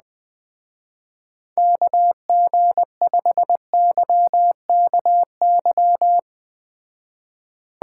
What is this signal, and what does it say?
Morse code, heard as a keyed tone, about 20 wpm, 700 Hz. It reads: KG5YKY
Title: Speed Racing - USA 2x3 Call Signs 17wpm